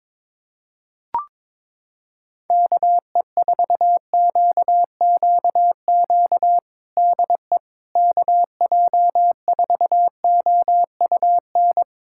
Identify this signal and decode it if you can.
Morse code, heard as a keyed tone, about 22 wpm, 700 Hz. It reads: KE4QQQ DE KJ4OUN
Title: USA Callsign DE Callsign - 6 Characters - 22wpm